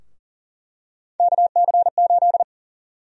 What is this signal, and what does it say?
Morse code, heard as a keyed tone, about 40 wpm, 700 Hz. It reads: KC8
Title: USA Call Sign Prefixes - 40wpm